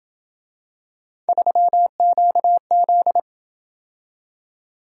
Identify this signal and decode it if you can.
Morse code, heard as a keyed tone, about 27 wpm, 700 Hz. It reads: 3QZ